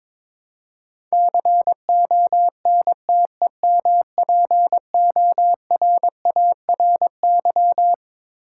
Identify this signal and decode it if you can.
Morse code, heard as a keyed tone, about 22 wpm, 700 Hz. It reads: CONTEMPORARY
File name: Twelve Letter Words - 22wpm